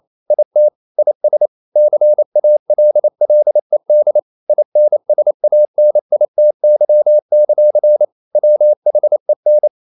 Morse code, heard as a keyed tone, about 28 wpm, 600 Hz. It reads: IT IS CALLED INSANITY; WHEN
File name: CW / Practice / EnglishQuotations0128